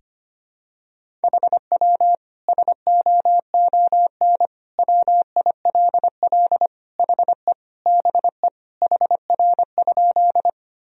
Morse code, heard as a keyed tone, about 25 wpm, 700 Hz. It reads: HW SOON WILL HE BE HR?
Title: Sentences from Top 400 Words and Abbreviations 25wpm